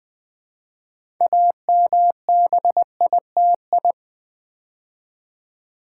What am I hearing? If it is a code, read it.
Morse code, heard as a keyed tone, about 20 wpm, 700 Hz. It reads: AMBITI
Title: Build A Word - 3 Letters At A Time - 9-Letter Words - 20wpm